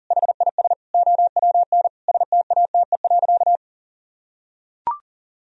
Morse code, heard as a keyed tone, about 37 wpm, 700 Hz. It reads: HIS OWN STATE.
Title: Sentences from Top 1000 Words - Review Part 3 - 40wpm